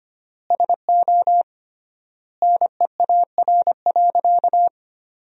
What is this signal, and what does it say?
Morse code, heard as a keyed tone, about 25 wpm, 700 Hz. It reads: SO NEAR.
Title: 3x Word-Spacing - Sentences from Top 300 Words - 25wpm